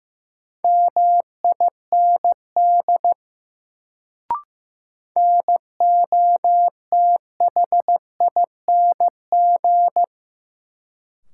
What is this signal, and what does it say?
Morse code, heard as a keyed tone, about 15 wpm, 700 Hz. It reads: MIND NOTHING
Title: Sentences from Top 500 Words - New Word Drill - 15wpm